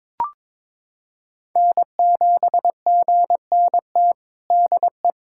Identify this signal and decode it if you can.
Morse code, heard as a keyed tone, about 22 wpm, 700 Hz. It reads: N7GNT DE
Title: USA Callsign DE Callsign 22wpm